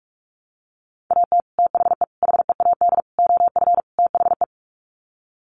Morse code, heard as a keyed tone, about 44 wpm, 700 Hz. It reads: AT THE HEAD OF THE
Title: Sets of 5 Words 45wpm